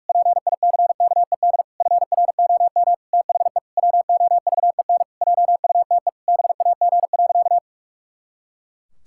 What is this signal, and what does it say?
Morse code, heard as a keyed tone, about 46 wpm, 700 Hz. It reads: PICKED FROM THE WOVEN JUTE BAG.
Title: Pangram Challenge v2 - 45wpm